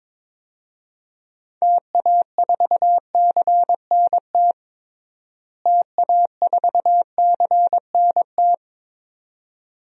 Morse code, heard as a keyed tone, about 22 wpm, 700 Hz. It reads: TA4CNT TA4CNT
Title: DXCC-100 Call Signs - 6 Characters - Repeated 2x - 22wpm